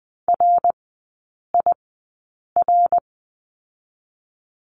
Morse code, heard as a keyed tone, about 20 wpm, spaced as for 8 wpm, 700 Hz. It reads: RIR
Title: 053 Lesson 10e - Letter r - Three Characters 20wpm